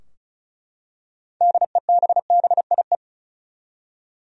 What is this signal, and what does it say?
Morse code, heard as a keyed tone, about 34 wpm, 700 Hz. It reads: DEBBIE
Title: USA Common Names - 35wpm